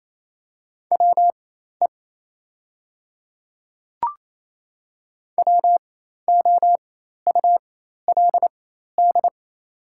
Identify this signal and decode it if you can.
Morse code, heard as a keyed tone, about 28 wpm, spaced as for 13 wpm, 700 Hz. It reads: WE WOULD
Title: 4x Character Spacing - Top 100 Words - 28wpm